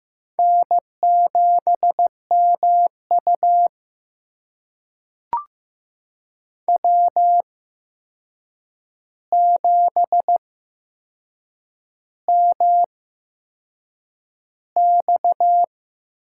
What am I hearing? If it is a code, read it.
Morse code, heard as a keyed tone, about 15 wpm, spaced as for 6 wpm, 700 Hz. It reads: N7MU W7MX
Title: USA Call Signs - 4 Characters - 8x Character Spacing - 15wpm